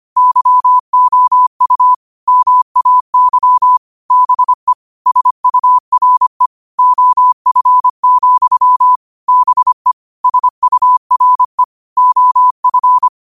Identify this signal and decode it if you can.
Morse code, heard as a keyed tone, about 25 wpm, 1000 Hz. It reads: YOU MAY BE SURE OF, BE SURE OF
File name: qotdfiles / 25wpm